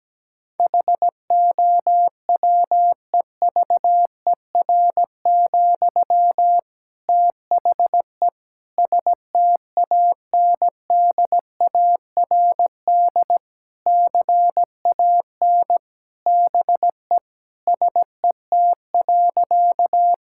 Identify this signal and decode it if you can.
Morse code, heard as a keyed tone, about 17 wpm, 700 Hz. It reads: HOWEVER, THE STANDARD CAN BE SET.
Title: Sentences from Top 1000 Words - Review Part 6 - 17wpm